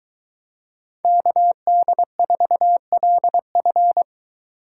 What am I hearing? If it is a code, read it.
Morse code, heard as a keyed tone, about 23 wpm, 700 Hz. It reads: KD4LF